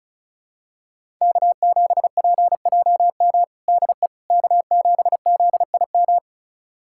Morse code, heard as a keyed tone, about 36 wpm, 700 Hz. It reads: K7PJM DE K7ZIM
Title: USA Callsign DE Callsign - 5 Characters - 35wpm